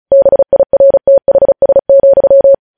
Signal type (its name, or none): Morse code